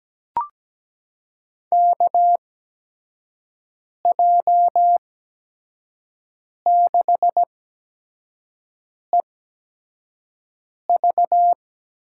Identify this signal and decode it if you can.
Morse code, heard as a keyed tone, about 17 wpm, spaced as for 5 wpm, 700 Hz. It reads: KJ6EV
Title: USA Call Signs - 6 Characters - 8x Character Spacing - 17wpm